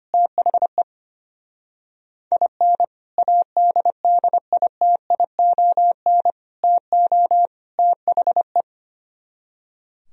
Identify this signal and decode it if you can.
Morse code, heard as a keyed tone, about 31 wpm, 700 Hz. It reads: THE IN ADDITION TO THE
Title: Speed-Racing - Sets of 4 Words 25wpm